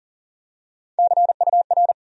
Morse code, heard as a keyed tone, about 40 wpm, 700 Hz. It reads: CUR
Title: USA Call Sign Postfixes - 3 Letters 40wpm